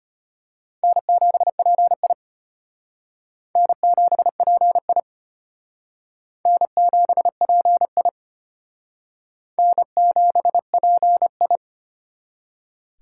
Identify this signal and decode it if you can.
Morse code, heard as a keyed tone, about 33 wpm, 700 Hz. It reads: N7PI N7PI N7PI N7PI